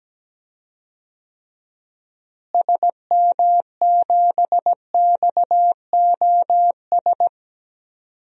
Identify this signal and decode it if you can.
Morse code, heard as a keyed tone, about 17 wpm, 700 Hz. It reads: SM7XOS
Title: DXCC-100 Call Signs - 6 Characters - Repeated 2x - 17wpm